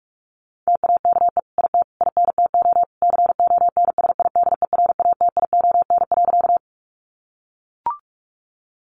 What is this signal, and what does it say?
Morse code, heard as a keyed tone, about 44 wpm, 700 Hz. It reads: TAKE IT INTO CONSIDERATION.
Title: Sentences from Top 1500 Words 45wpm